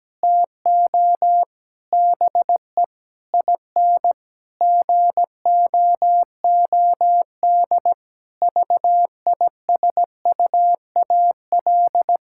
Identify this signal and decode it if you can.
Morse code, heard as a keyed tone, about 17 wpm, 700 Hz. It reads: TO BE IN GOOD VISUAL
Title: Sentences from Top 1000 Words - Review Part 6 - 17wpm